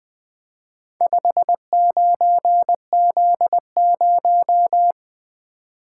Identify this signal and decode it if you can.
Morse code, heard as a keyed tone, about 20 wpm, 700 Hz. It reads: 59Z0
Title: Four Letter-Number Combinations - 20wpm